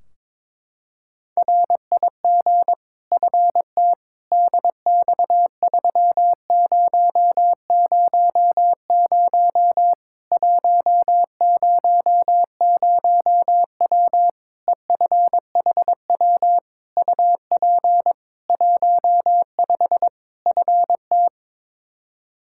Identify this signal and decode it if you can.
Morse code, heard as a keyed tone, about 22 wpm, 700 Hz. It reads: RIG FT DX3000 100W EFHW UP 15 FT
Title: Practice QSO - Exchange 2 - Info - 22wpm